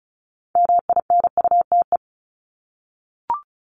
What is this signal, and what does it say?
Morse code, heard as a keyed tone, about 34 wpm, 700 Hz. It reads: MINUTE
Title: Top 500 Words Repeated 35wpm